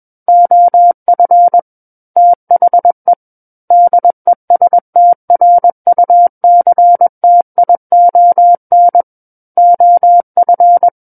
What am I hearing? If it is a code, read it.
Morse code, heard as a keyed tone, about 21 wpm, 700 Hz. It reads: OF THE DESTRUCTION OF
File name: Book One / 12-Chapter 12 - 21 WPM